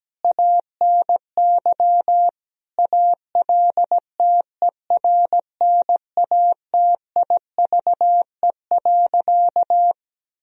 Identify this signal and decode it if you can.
Morse code, heard as a keyed tone, about 17 wpm, 700 Hz. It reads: ANY ALTERNATIVE.
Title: Sentences from Top 1000 Words - Review Part 4 - 17wpm